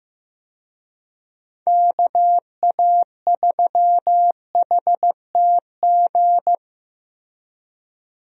Morse code, heard as a keyed tone, about 15 wpm, 700 Hz. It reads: KA3HTG
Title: USA Call Signs - 6 Characters - Repeated 3x - 15wpm